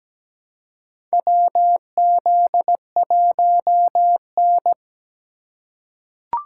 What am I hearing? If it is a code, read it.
Morse code, heard as a keyed tone, about 17 wpm, 700 Hz. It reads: WZ1N
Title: USA Call Signs - 4 Characters - 4x Character Spacing - 17wpm